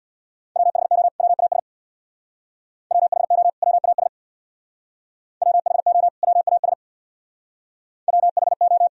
Morse code, heard as a keyed tone, about 61 wpm, 700 Hz. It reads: WHO WAS WHO WAS WHO WAS WHO
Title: Speed-Racing - Sets of 2 Words 50wpm